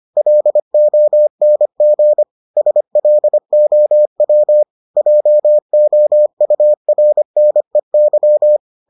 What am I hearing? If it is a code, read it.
Morse code, heard as a keyed tone, about 25 wpm, 600 Hz. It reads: LONG SLOW JOURNEY